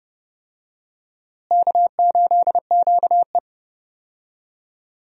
Morse code, heard as a keyed tone, about 32 wpm, 700 Hz. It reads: K8QE
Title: USA Call Signs - 4 Characters - Repeated 2x - 30wpm